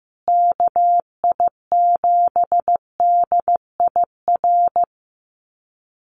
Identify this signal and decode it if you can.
Morse code, heard as a keyed tone, about 15 wpm, 700 Hz. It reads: KI7DIR
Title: USA Call Signs Repeated 3x 15wpm